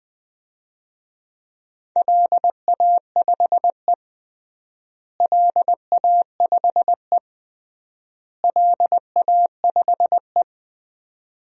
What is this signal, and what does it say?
Morse code, heard as a keyed tone, about 20 wpm, 700 Hz. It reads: LA5E LA5E LA5E
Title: DXCC-100 Call Signs - 4 Characters - Repeated 3x - 20wpm